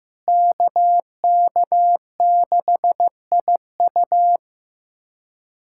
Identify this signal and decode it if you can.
Morse code, heard as a keyed tone, about 15 wpm, 700 Hz. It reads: KK6IU
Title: USA Callsign DE Callsign - 5 Characters - 15wpm